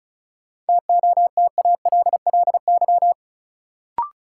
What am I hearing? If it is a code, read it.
Morse code, heard as a keyed tone, about 36 wpm, 700 Hz. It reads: TOTALLY
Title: Sentences from Top 1600 Words - New Word Drill Alt - 35wpm